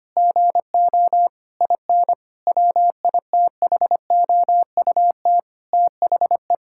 Morse code, heard as a keyed tone, about 25 wpm, 700 Hz. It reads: GO IN WITHOUT THE
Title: Sentences from Top 1000 Words - Review Part 6 - 25wpm